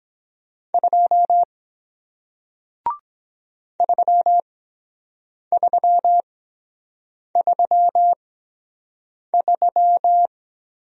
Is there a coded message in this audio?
Yes: Morse code from 0.7 to 10.3 s, heard as a keyed tone, about 23 wpm, spaced as for 7 wpm, 700 Hz. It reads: 2 3333